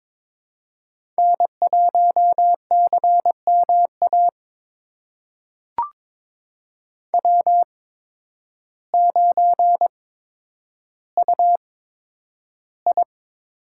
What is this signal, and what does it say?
Morse code, heard as a keyed tone, about 22 wpm, spaced as for 10 wpm, 700 Hz. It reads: N1CMA W9UI
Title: USA Call Signs - 8x Character Spacing - 22wpm